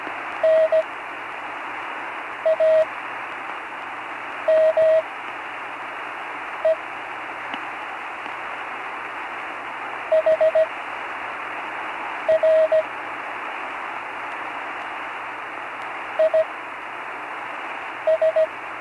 Morse code, heard as a keyed tone, about 17 wpm, spaced as for 5 wpm, 630 Hz, with radio noise. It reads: NAME HR IS
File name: CW Learning Programs / Learning Morse Code (from Tune in the World) / 35 answers